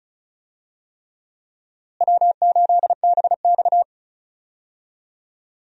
Morse code, heard as a keyed tone, about 34 wpm, 700 Hz. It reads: W8BX
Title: USA Call Signs - 4 Characters - Repeated 3x - 35wpm